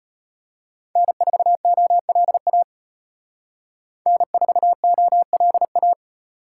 Morse code, heard as a keyed tone, about 32 wpm, 700 Hz. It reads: N4OLA N4OLA